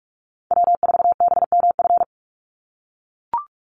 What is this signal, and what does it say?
Morse code, heard as a keyed tone, about 44 wpm, 700 Hz. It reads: W4BMF
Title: USA Call Signs 45wpm